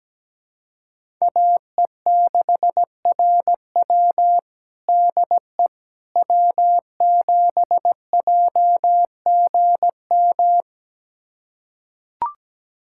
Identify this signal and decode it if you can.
Morse code, heard as a keyed tone, about 17 wpm, 700 Hz. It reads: AE6RW DE W7JGM
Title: USA Callsign DE Callsign - 5 Characters - 17wpm